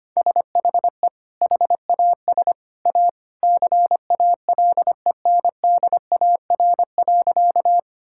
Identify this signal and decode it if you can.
Morse code, heard as a keyed tone, about 25 wpm, 700 Hz. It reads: SHE HAS A CALENDAR.
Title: Sentences from Top 1600 Words 25wpm